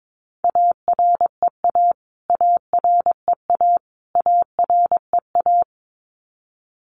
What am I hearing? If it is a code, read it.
Morse code, heard as a keyed tone, about 22 wpm, 700 Hz. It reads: AREA AREA AREA
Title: Top 500 Words Repeated 22wpm